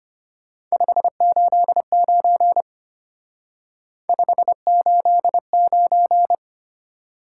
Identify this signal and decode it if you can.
Morse code, heard as a keyed tone, about 31 wpm, 700 Hz. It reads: 589 589
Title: Speed-Racing Top 180 QSO Elements 25wpm